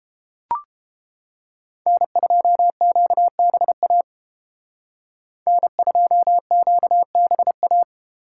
Morse code, heard as a keyed tone, about 29 wpm, 700 Hz. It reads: N2QBA N2QBA